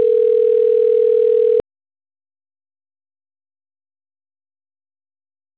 Ringback tone (North America), 440 + 480 Hz.
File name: autotestagent / 11-27-00.136..callcentertransfer. .32fa30d4-2555-42ec-b2ec